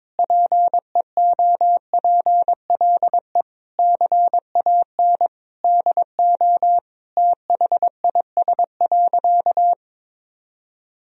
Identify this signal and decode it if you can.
Morse code, heard as a keyed tone, about 22 wpm, 700 Hz. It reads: PEOPLE CAN DO THIS.
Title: Sentences from Top 1000 Words - Review Part 1 - 22wpm